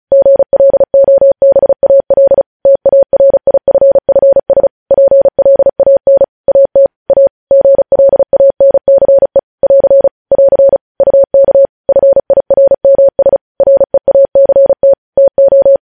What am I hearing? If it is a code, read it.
Morse code, heard as a keyed tone, about 36 wpm, 550 Hz. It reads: GLOBAL TARIFFS PLAN AT A GLANCE <AR> <AR> UK FIRMS REACT TO
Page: news headlines in morse code at 35 wpm